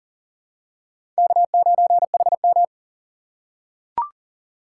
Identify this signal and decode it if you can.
Morse code, heard as a keyed tone, about 39 wpm, 700 Hz. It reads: K9HM
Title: USA Call Signs - 3x Character Spacing - 40wpm